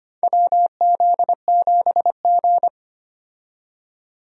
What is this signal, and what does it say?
Morse code, heard as a keyed tone, about 25 wpm, 700 Hz. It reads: WZ7G
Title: USA Call Signs - 4 Characters - 25wpm